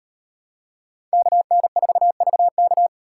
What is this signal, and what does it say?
Morse code, heard as a keyed tone, about 38 wpm, 700 Hz. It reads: KN4VK